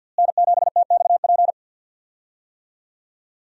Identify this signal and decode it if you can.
Morse code, heard as a keyed tone, about 52 wpm, 700 Hz. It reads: N7TXP
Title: USA Call Signs - 5 Characters - Repeated 2x - 50wpm